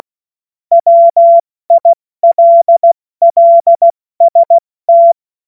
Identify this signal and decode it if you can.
Morse code, heard as a keyed tone, about 16 wpm, 680 Hz. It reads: WILLST
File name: Morsekurs / Witze / Witz 16. November 2021, 70 F80